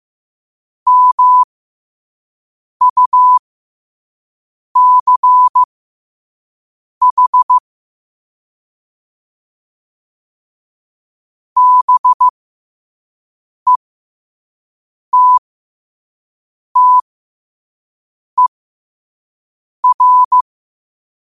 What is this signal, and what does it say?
Morse code, heard as a keyed tone, about 15 wpm, spaced as for 5 wpm, 1000 Hz. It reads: MUCH BETTER
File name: qotdfiles / 5wpm